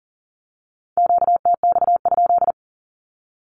Morse code, heard as a keyed tone, about 39 wpm, 700 Hz. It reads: QTX?
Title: Q-Codes with Spoken Meaning - 40wpm